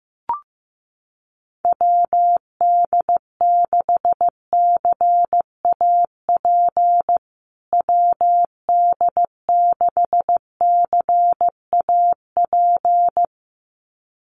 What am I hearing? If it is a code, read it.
Morse code, heard as a keyed tone, about 15 wpm, 700 Hz. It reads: WD6CAP WD6CAP
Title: USA Call Signs Repeated 2x 15wpm